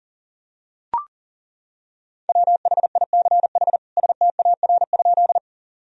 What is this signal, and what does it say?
Morse code, heard as a keyed tone, about 40 wpm, 700 Hz. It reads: WHICH STAR?
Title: Sentences from Top 1000 Words - Review Part 2 - 40wpm